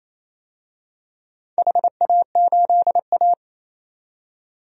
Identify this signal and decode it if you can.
Morse code, heard as a keyed tone, about 27 wpm, 700 Hz. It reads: HA8A
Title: DXCC-100 Call Signs - 4 Characters - 28wpm